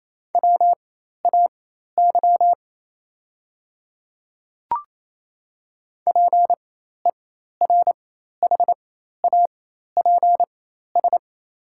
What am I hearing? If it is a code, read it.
Morse code, heard as a keyed tone, about 28 wpm, spaced as for 13 wpm, 700 Hz. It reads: WAY PERHAPS
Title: 4x Character Spacing - Top 500 Words - 28wpm